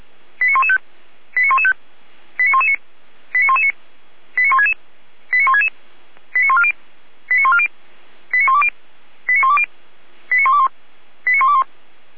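Five-tone selective call (ZVEI), heard as quick runs of five tones, tones of about 70 ms. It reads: 89115 89115 89119 89119 89166 89166 89140 89140 89120 89120 89121 89121
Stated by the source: Five-tone ZVEI selective call 89073, received by SDR, from Feuerwehr Orlamünde.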